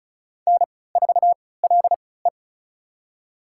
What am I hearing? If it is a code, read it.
Morse code, heard as a keyed tone, about 34 wpm, spaced as for 20 wpm, 700 Hz. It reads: N4LE
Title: USA Call Signs - 4 Characters - 3x Character Spacing - 35wpm